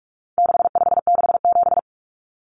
Morse code, heard as a keyed tone, about 44 wpm, 700 Hz. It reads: 6567
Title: Numbers - 4-Digit 45wpm